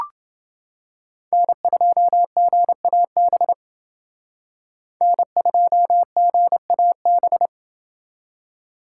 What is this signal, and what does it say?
Morse code, heard as a keyed tone, about 28 wpm, 700 Hz. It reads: N2GAB N2GAB